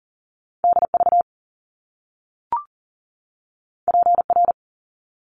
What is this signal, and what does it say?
Morse code, heard as a keyed tone, about 38 wpm, 700 Hz. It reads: DV PR
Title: Top Two Letter Combinations in Words 40wpm